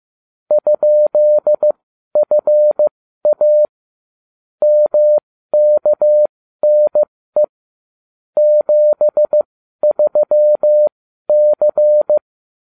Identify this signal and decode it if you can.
Morse code, heard as a keyed tone, about 15 wpm, spaced as for 12 wpm, 600 Hz. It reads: ?FA MKNE 73C
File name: cw / ubacw35 3